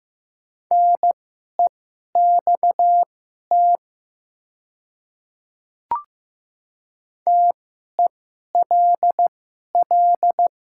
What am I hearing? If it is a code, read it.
Morse code, heard as a keyed tone, about 15 wpm, spaced as for 11 wpm, 700 Hz. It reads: NEXT TELL
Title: 2x Character Spacing - Top 300 Words - 15wpm